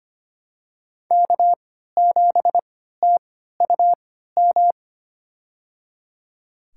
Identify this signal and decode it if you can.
Morse code, heard as a keyed tone, about 25 wpm, spaced as for 14 wpm, 700 Hz. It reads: K7TUM
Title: USA Call Signs - 5 Characters - 3x Character Spacing - 25wpm